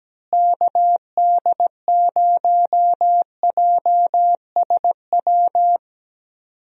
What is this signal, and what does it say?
Morse code, heard as a keyed tone, about 17 wpm, 700 Hz. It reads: KD0JSW
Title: USA Call Signs - 6 Characters - Repeated 3x - 17wpm